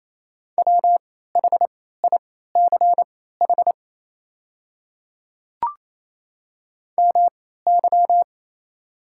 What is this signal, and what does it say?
Morse code, heard as a keyed tone, about 28 wpm, spaced as for 16 wpm, 700 Hz. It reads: WHICH MY
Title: 3x Character Spacing - Top 300 Words - 28wpm